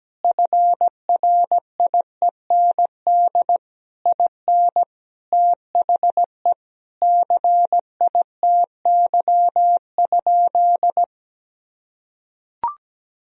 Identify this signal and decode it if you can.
Morse code, heard as a keyed tone, about 17 wpm, 700 Hz. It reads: FRIEND IN THE CITY?
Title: Sentences from Top 1000 Words - Review Part 1 - 17wpm